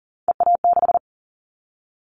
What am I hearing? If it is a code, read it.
Morse code, heard as a keyed tone, about 42 wpm, 700 Hz. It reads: EA6
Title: Callsign Prefixes for Top 100 DXCC - 40wpm